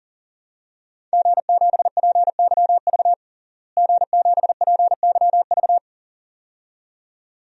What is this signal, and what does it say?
Morse code, heard as a keyed tone, about 40 wpm, 700 Hz. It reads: G7PYV G7PYV
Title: DXCC-100 Call Signs - 5 Characters - Repeated 2x - 40wpm